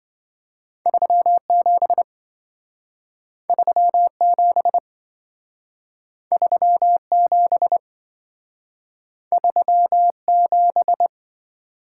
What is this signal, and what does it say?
Morse code, heard as a keyed tone, about 26 wpm, 700 Hz. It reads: 37 37 37 37